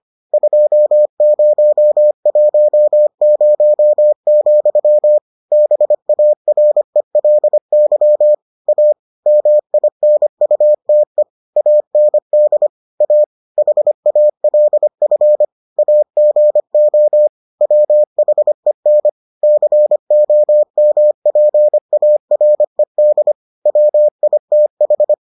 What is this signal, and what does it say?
Morse code, heard as a keyed tone, about 25 wpm, 600 Hz. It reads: 2010, BARELY A MINUTE AND A HALF AGO WHEN COMPARED WITH